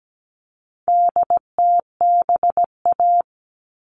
Morse code, heard as a keyed tone, about 17 wpm, 700 Hz. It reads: DTBA
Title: All Cut Numbers - 4-Digit 17wpm